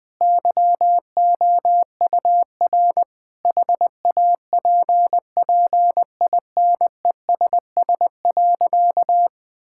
Morse code, heard as a keyed tone, about 21 wpm, 700 Hz. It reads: YOUR HAPPINESS.
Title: Sentences with 6-Word Phrases - 20wpm